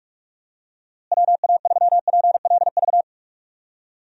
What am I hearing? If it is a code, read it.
Morse code, heard as a keyed tone, about 41 wpm, 700 Hz. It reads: WA3PLV
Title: USA Call Signs - 6 Characters - Repeated 2x - 45wpm